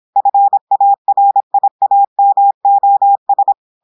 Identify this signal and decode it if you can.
Morse code, heard as a keyed tone, about 26 wpm, 800 Hz. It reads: FARIAMOS
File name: audios / portugais / portugais texte en clair 1 vitesse 26 mots minutes